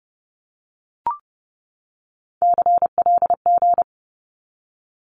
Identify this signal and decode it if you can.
Morse code, heard as a keyed tone, about 30 wpm, 700 Hz. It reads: CLG
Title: Abbreviations 30wpm